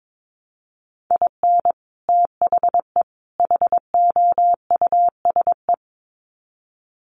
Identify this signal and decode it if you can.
Morse code, heard as a keyed tone, about 22 wpm, 700 Hz. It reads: IN THE HOUSE
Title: Sets of 3 Words 22wpm